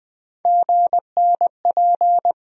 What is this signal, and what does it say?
Morse code, heard as a keyed tone, about 20 wpm, 700 Hz. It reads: GNP
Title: USA Call Sign Postfixes - 3 Letters 20wpm